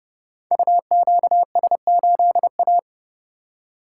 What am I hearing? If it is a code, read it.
Morse code, heard as a keyed tone, about 30 wpm, 700 Hz. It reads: UQS8A
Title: Code Groups - 5 Letter:Numbers - 30wpm